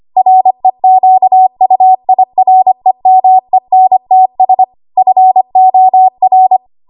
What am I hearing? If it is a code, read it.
Morse code, heard as a keyed tone, about 25 wpm, 750 Hz. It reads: REQUIREMENTS FOR
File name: Archive / 25 WPM / 25WPM